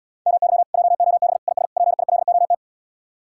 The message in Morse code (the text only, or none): MY JOB IS PILOT